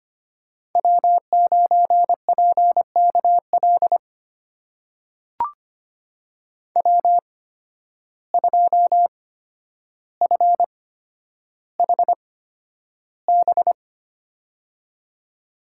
Morse code, heard as a keyed tone, about 25 wpm, spaced as for 11 wpm, 700 Hz. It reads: W9PKL W2FHB
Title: USA Call Signs - 5 Characters - 8x Character Spacing - 25wpm